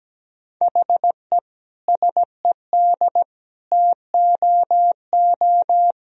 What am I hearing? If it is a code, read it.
Morse code, heard as a keyed tone, about 17 wpm, 700 Hz. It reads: HE SED TOO
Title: Sentences from Top 200 Words and Abbreviations 17wpm